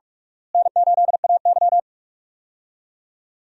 Morse code, heard as a keyed tone, about 46 wpm, 700 Hz. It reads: N8AY
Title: USA Call Signs - 4 Characters - Repeated 3x - 45wpm